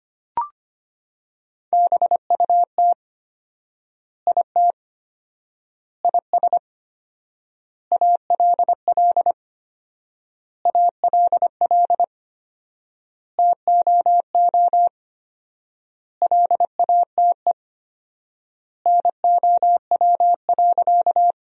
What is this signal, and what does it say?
Morse code, heard as a keyed tone, about 25 wpm, 700 Hz. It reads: BUT IT IS ALL ALL TOO LATE NOW.
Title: 4x Word-Spacing - Sentences from Top 300 Words - 25wpm